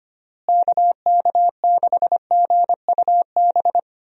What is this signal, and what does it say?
Morse code, heard as a keyed tone, about 25 wpm, 700 Hz. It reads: KK6GUB
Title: USA Call Signs - 6 Characters - Repeated 3x - 25wpm